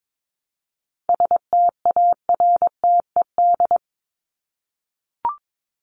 Morse code, heard as a keyed tone, about 22 wpm, 700 Hz. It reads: STARTED
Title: ED Suffix - Top 100 Words with Base Words 22wpm